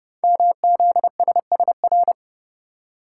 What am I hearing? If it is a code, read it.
Morse code, heard as a keyed tone, about 30 wpm, 700 Hz. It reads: MZSSR
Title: Code Groups - 5 Letters - 30wpm